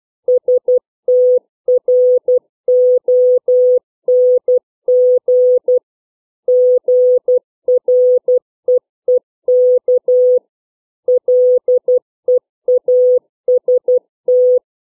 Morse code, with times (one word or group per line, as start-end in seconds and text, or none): 0.3-5.8 STRONG
6.5-10.4 GREEK
11.1-14.6 LEAST